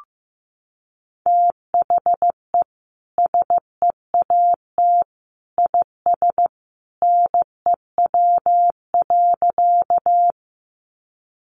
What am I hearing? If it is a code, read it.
Morse code, heard as a keyed tone, about 15 wpm, 700 Hz. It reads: THE SEAT IS NEW.
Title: Sentences from Top 1500 Words 15wpm